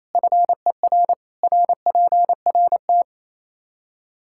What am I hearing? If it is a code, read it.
Morse code, heard as a keyed tone, about 27 wpm, 700 Hz. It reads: FER RPRT
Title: Practice QSO - Exchange 2 - Info - 28wpm